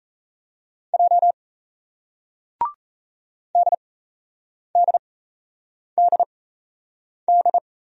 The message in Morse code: J DDDD

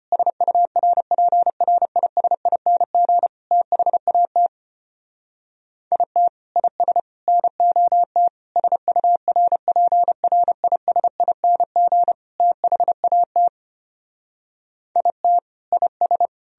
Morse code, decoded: SURPRISING THAT IT IS NOT SURPRISING THAT IT IS

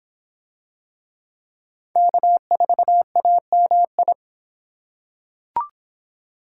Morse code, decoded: K4AMI

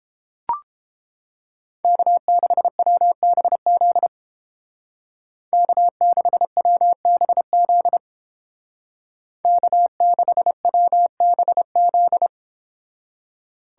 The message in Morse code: K6WBZ K6WBZ K6WBZ

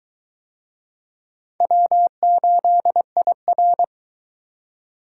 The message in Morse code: W8IR

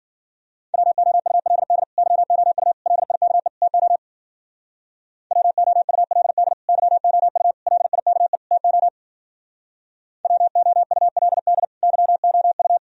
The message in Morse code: WOULD YOU LIKE TO WOULD YOU LIKE TO WOULD YOU

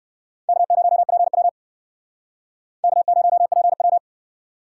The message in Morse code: K0PW K0PW

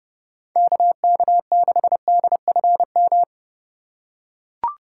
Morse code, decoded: KK6DFM

